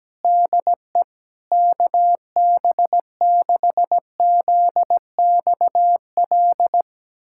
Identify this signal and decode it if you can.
Morse code, heard as a keyed tone, about 17 wpm, 700 Hz. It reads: DE KB6ZXL
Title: USA Callsign DE Callsign - 6 Characters - 17wpm